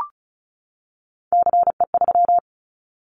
Morse code, heard as a keyed tone, about 34 wpm, 700 Hz. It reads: CE3
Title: Callsign Prefixes for Top 100 DXCC - 35wpm